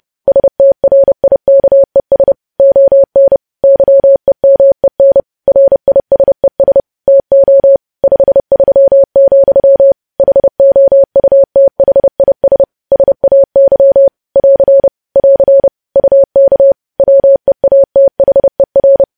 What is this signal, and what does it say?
Morse code, heard as a keyed tone, about 30 wpm, 550 Hz. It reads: STRIKES ON YEMEN RISES TO 53, HOUTHIS SAY <AR> <AR> UK WEATHER
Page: News Headlines in Morse Code at 30 WPM